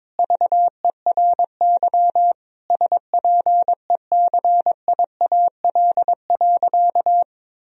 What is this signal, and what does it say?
Morse code, heard as a keyed tone, about 22 wpm, 700 Hz. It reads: VERY SPECIAL.
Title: Sentences from Top 1000 Words - Review Part 3 - 22wpm